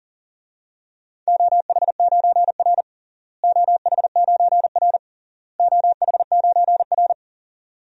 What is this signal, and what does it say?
Morse code, heard as a keyed tone, about 40 wpm, 700 Hz. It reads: OH9R OH9R OH9R
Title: DXCC-100 Call Signs - 4 Characters - Repeated 3x - 40wpm